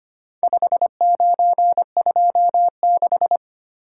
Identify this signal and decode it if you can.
Morse code, heard as a keyed tone, about 25 wpm, 700 Hz. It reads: 5926
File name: practice / 25wpm / Four Letter-Number Combinations - 25wpm